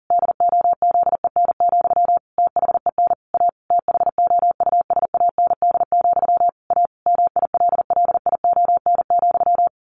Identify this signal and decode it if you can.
Morse code, heard as a keyed tone, about 40 wpm, 700 Hz. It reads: DOZEN, THEN A THOUSAND, A MILLION,
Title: A Sound Of Thunder 40wpm